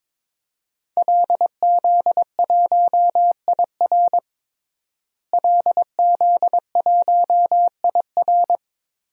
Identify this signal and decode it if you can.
Morse code, heard as a keyed tone, about 22 wpm, 700 Hz. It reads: LZ1IR LZ1IR
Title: DXCC-100 Call Signs - 5 Characters - Repeated 3x - 22wpm